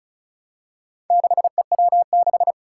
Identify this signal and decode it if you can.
Morse code, heard as a keyed tone, about 36 wpm, 700 Hz. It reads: 6EW6
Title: Four Challenging Letter-Number Combinations 35wpm